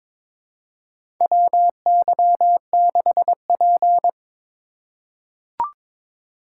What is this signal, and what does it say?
Morse code, heard as a keyed tone, about 21 wpm, 700 Hz. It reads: WY6P
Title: USA Call Signs - 4 Characters - 2x Character Spacing - 22wpm